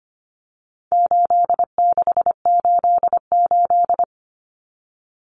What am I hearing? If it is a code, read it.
Morse code, heard as a keyed tone, about 25 wpm, 700 Hz. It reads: 8688
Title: Numbers - 4-Digit 25wpm